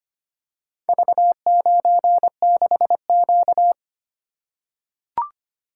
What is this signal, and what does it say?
Morse code, heard as a keyed tone, about 25 wpm, 700 Hz. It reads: V96Q
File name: practice / 25wpm / Four Letter-Number Combinations - 25wpm